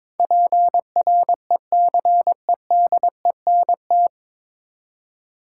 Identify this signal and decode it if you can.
Morse code, heard as a keyed tone, about 22 wpm, 700 Hz. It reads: PRECEDENT
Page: Morse Code Ninja practice 22wpm